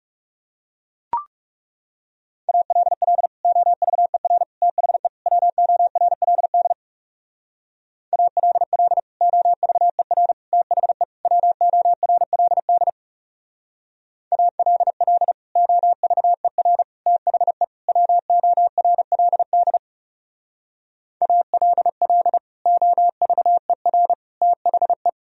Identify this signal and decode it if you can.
Morse code, heard as a keyed tone, about 35 wpm, 700 Hz. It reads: ALL OVER THE WORLD ALL OVER THE WORLD ALL OVER THE WORLD ALL OVER THE
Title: Speed-Racing - Sets of 4 Words 30wpm